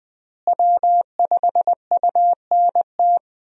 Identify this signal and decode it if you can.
Morse code, heard as a keyed tone, about 20 wpm, 700 Hz. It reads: W5UNT
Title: USA Call Signs - 5 Characters - Repeated 3x - 20wpm